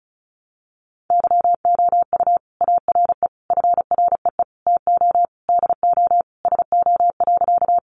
Morse code, heard as a keyed tone, about 34 wpm, 700 Hz. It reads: YOU ARE FREE TO DO SO.
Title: Sentences from Top 500 Words 35wpm